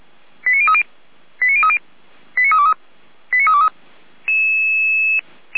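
Five-tone selective call (ZVEI), heard as quick runs of five tones, tones of about 70 ms. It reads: 89030 89030 89323 89323
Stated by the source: ZVEI five-tone call received by SDR, archived by Feuerwehr Orlamünde (fire brigade).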